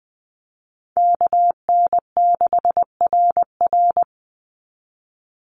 Morse code, heard as a keyed tone, about 19 wpm, 700 Hz. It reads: KN6RR
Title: USA Call Signs 20wpm